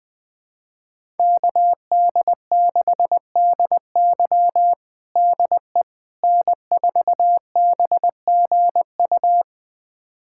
Morse code, heard as a keyed tone, about 21 wpm, 700 Hz. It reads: KD6DY DE N4BGU
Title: USA Callsign DE Callsign - 5 Characters - 20wpm